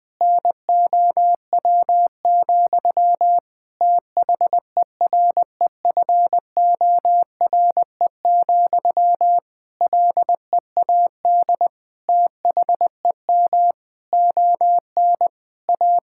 Morse code, heard as a keyed tone, about 20 wpm, 700 Hz. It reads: NOW, THEREFORE, LEAD THEM ON A
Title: Sentences from Top 1000 Words - Review Part 5 - 20wpm